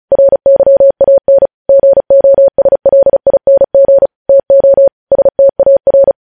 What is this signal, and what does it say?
Morse code, heard as a keyed tone, about 36 wpm, 550 Hz. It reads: RYAN GOSLING TO STAR
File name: podcast / News headlines-1x-35x35